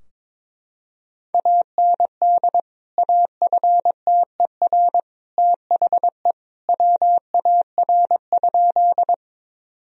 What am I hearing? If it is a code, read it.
Morse code, heard as a keyed tone, about 21 wpm, 700 Hz. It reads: AND AFTER THE WAR?
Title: Sentences from Top 1000 Words - Review Part 2 - 22wpm